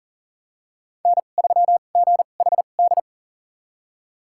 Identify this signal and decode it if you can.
Morse code, heard as a keyed tone, about 38 wpm, spaced as for 28 wpm, 700 Hz. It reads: N3GHD
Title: USA Call Signs - 5 Characters - 2x Character Spacing - 40wpm